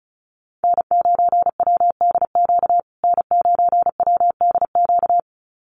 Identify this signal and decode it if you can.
Morse code, heard as a keyed tone, about 34 wpm, 700 Hz. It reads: N9WDQ N9WDQ
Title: USA Call Signs Repeated 2x 35wpm